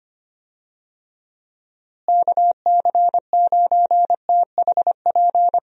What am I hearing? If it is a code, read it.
Morse code, heard as a keyed tone, about 25 wpm, 700 Hz. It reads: KC9THP
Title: USA Call Signs - 6 Characters - Repeated 2x - 25wpm